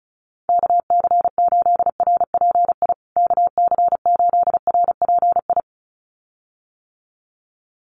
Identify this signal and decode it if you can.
Morse code, heard as a keyed tone, about 34 wpm, 700 Hz. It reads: KC8RPI KC8RPI
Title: USA Call Signs Repeated 2x 35wpm